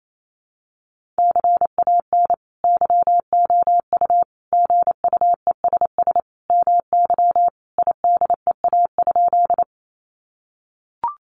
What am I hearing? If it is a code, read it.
Morse code, heard as a keyed tone, about 28 wpm, 700 Hz. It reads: CAN YOU GUESS MY IDEA?
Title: Sentences from Top 1500 Words 28wpm